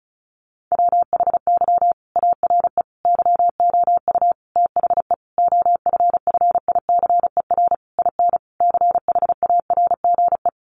Morse code, heard as a keyed tone, about 34 wpm, 700 Hz. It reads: WHY ARE YOU THE OFFICER IN CHARGE
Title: Sentences from Top 1200 Words 35wpm